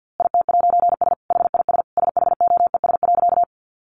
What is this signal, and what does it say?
Morse code, heard as a keyed tone, about 50 wpm, 700 Hz. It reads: IT'S HIS SHOES.
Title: A Sound Of Thunder 50wpm